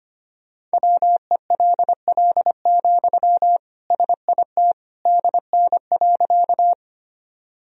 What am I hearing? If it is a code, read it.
Morse code, heard as a keyed tone, about 25 wpm, 700 Hz. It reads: WELL, SIT DN.
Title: Sentences from Top 500 Words and Abbreviations 25wpm